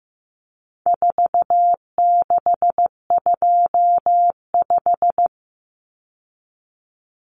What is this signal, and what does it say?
Morse code, heard as a keyed tone, about 15 wpm, 700 Hz. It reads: 4625
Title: Common Cut Numbers - 4-Digit 15wpm